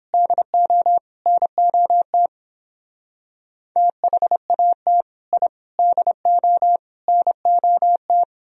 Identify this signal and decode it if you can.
Morse code, heard as a keyed tone, about 29 wpm, 700 Hz. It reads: DO NOT THAT I DO NOT